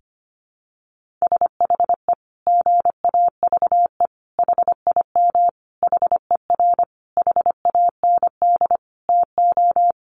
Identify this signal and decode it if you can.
Morse code, heard as a keyed tone, about 25 wpm, 700 Hz. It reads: SHE GAVE HIM HER HAND TO
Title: Sentences from Top 1400 Words 25wpm